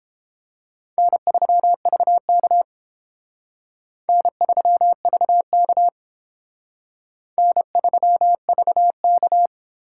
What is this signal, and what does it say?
Morse code, heard as a keyed tone, about 30 wpm, 700 Hz. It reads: N3VK N3VK N3VK